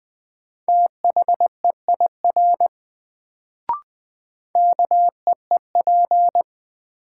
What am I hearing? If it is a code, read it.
Morse code, heard as a keyed tone, about 20 wpm, 700 Hz. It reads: THEIR KEEP
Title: Sentences from Top 300 Words - New Word Drill - 20wpm